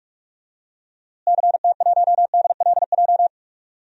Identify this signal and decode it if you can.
Morse code, heard as a keyed tone, about 43 wpm, 700 Hz. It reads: CT1DLJ
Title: DXCC-100 Call Signs - 6 Characters - Repeated 3x - 45wpm